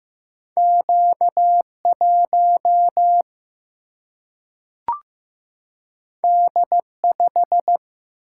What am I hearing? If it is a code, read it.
Morse code, heard as a keyed tone, about 15 wpm, 700 Hz. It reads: Q1 D5
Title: Single-Letter Single-Number 15wpm